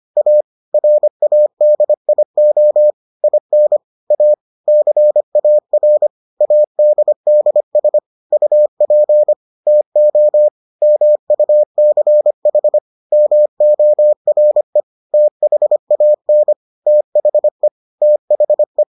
Morse code, heard as a keyed tone, about 25 wpm, 600 Hz. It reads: A RADIO IN A CAR ADDS UP TO MUCH MORE THAN THE THE